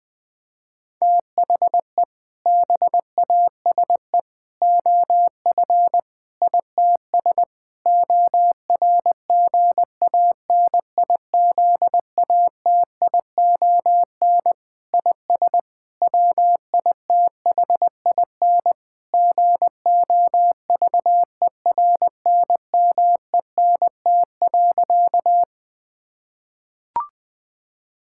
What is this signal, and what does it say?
Morse code, heard as a keyed tone, about 21 wpm, 700 Hz. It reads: THE BASE OF ITS ORGANIZATION IS WITHIN GOVERNMENT.
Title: Sentences from Top 1000 Words - Review Part 5 - 20wpm